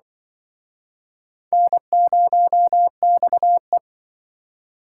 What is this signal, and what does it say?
Morse code, heard as a keyed tone, about 24 wpm, 700 Hz. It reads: N0XE
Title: Speed Racing - USA 1x2 Call Signs 20wpm